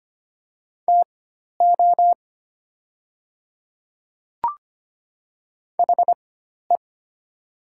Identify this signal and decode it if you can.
Morse code, heard as a keyed tone, about 25 wpm, spaced as for 12 wpm, 700 Hz. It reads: TO HE
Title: 4x Character Spacing - Top 200 Words - 25wpm